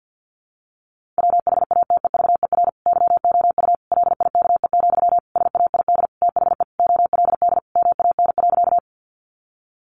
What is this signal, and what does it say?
Morse code, heard as a keyed tone, about 50 wpm, 700 Hz. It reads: WHATEVER YOU LIKE, SAID THE OLD MAN.
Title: Sentences from Top 1100 Words 50wpm